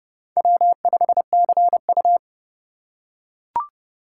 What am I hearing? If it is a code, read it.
Morse code, heard as a keyed tone, about 30 wpm, 700 Hz. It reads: W5CU
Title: USA Call Signs - 4 Characters - 3x Character Spacing - 30wpm